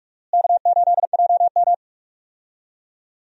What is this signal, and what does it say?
Morse code, heard as a keyed tone, about 46 wpm, 700 Hz. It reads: K8JM
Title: USA Call Signs - 4 Characters - Repeated 3x - 45wpm